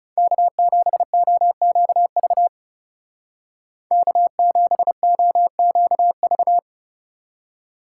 Morse code, heard as a keyed tone, about 37 wpm, 700 Hz. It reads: K7OQV K7OQV
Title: Speed Racing - USA 1x3 Call Signs 30wpm